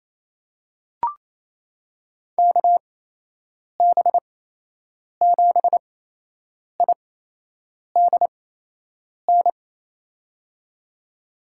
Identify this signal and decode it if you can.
Morse code, heard as a keyed tone, about 28 wpm, spaced as for 8 wpm, 700 Hz. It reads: KB7IDN
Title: USA Call Signs - 6 Characters - 8x Character Spacing - 28wpm